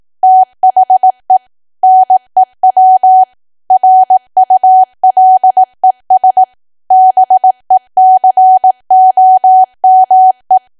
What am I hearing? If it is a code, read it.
Morse code, heard as a keyed tone, about 18 wpm, 750 Hz. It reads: THE NEW RULES BECOME
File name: cwa / W1AW-Level2 / 18WPM